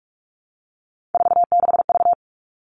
Morse code, heard as a keyed tone, about 44 wpm, 700 Hz. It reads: 46V
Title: All Cut Numbers - 3-Digit 45wpm